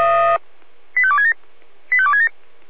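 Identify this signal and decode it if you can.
Five-tone selective call (ZVEI), heard as quick runs of five tones, tones of about 70 ms. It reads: 85267 85267
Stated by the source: Five-tone ZVEI selective call 85270, received by SDR.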